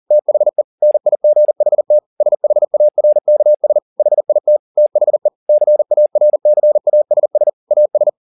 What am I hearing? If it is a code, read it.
Morse code, heard as a keyed tone, about 40 wpm, 600 Hz. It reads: THE NIGHT SHARKS HIT THE CARCASS AS
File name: the old man and the sea 2 40wpm